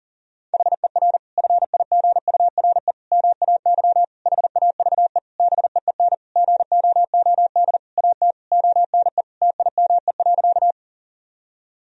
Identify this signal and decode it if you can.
Morse code, heard as a keyed tone, about 37 wpm, 700 Hz. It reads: HER FIGURE MAY HAVE BEEN GOOD AT ONE TIME.
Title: Sentences from Top 1000 Words - Review Part 2 - 40wpm